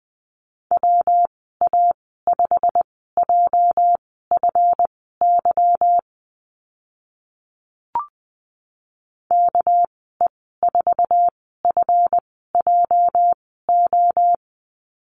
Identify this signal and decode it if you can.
Morse code, heard as a keyed tone, about 20 wpm, spaced as for 15 wpm, 700 Hz. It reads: WA5JFY KE4FJO
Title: 245 Lesson 39g - Letter z - Call Signs 20wpm